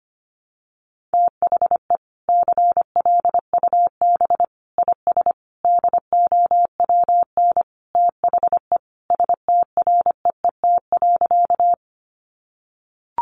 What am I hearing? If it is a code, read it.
Morse code, heard as a keyed tone, about 25 wpm, 700 Hz. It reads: THE CLUB IS DOWN THE STREET.
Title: Sentences from Top 1200 Words 25wpm